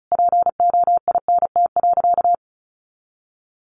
Morse code, heard as a keyed tone, about 34 wpm, 700 Hz. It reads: POINT.
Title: Sentences from Top 1400 Words 35wpm